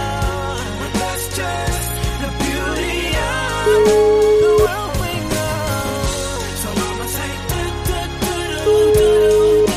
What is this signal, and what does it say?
Ringback tone (Europe, 425 Hz), with noise.